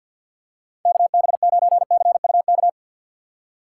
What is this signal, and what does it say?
Morse code, heard as a keyed tone, about 53 wpm, 700 Hz. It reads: KB9CUK
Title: USA Call Signs - 6 Characters - 6x Character Spacing - 50wpm